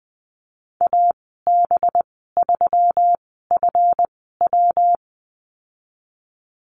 Morse code, heard as a keyed tone, about 20 wpm, spaced as for 14 wpm, 700 Hz. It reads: AB3FW
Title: 193 Lesson 31g - Letter v - Call Signs 20wpm